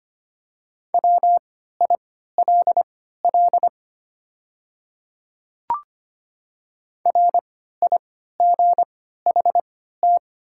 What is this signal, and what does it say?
Morse code, heard as a keyed tone, about 25 wpm, spaced as for 14 wpm, 700 Hz. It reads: WILL RIGHT
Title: 3x Character Spacing - Top 200 Words - 25wpm